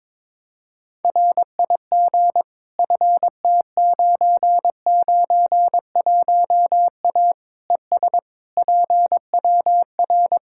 Morse code, heard as a keyed tone, about 22 wpm, 700 Hz. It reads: RIG FT991A ES PWR
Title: K1USN SST QSO Practice 22wpm